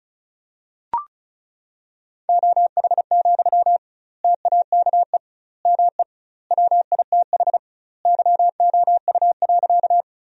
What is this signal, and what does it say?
Morse code, heard as a keyed tone, about 36 wpm, 700 Hz. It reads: OH, TAKE ME WITH YOU.
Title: 2x Word-Spacing - Sentences from Top 500 Words - 35wpm